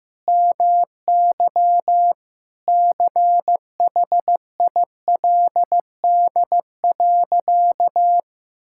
Morse code, heard as a keyed tone, about 16 wpm, 700 Hz. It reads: MY CHILD.
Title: Sentences from Top 1600 Words 15wpm